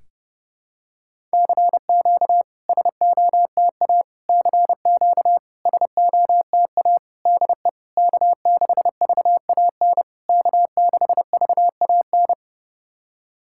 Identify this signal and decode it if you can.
Morse code, heard as a keyed tone, about 30 wpm, 700 Hz. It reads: CQ SOTA CQ SOTA DE K6VAN K6VAN
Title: SOTA QSO Practice 30wpm